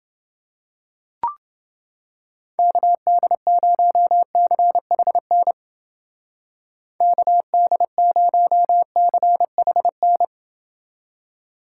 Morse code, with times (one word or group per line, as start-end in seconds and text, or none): 2.6-5.5 KD0CHN
7.0-10.2 KD0CHN